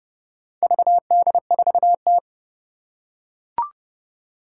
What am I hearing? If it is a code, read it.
Morse code, heard as a keyed tone, about 32 wpm, 700 Hz. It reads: VD4T
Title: Four Challenging Letter-Number Combinations 30wpm